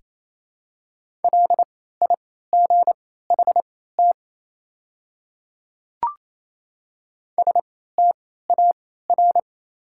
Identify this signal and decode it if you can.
Morse code, heard as a keyed tone, about 28 wpm, spaced as for 16 wpm, 700 Hz. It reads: LIGHT STAR
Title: 3x Character Spacing - Top 500 Words - 28wpm